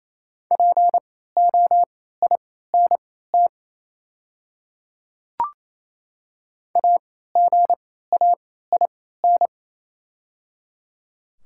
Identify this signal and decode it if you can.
Morse code, heard as a keyed tone, about 28 wpm, spaced as for 16 wpm, 700 Hz. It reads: POINT AGAIN
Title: 3x Character Spacing - Top 200 Words - 28wpm